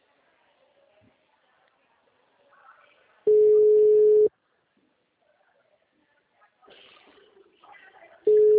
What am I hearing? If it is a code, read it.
Ringback tone (Europe, 425 Hz).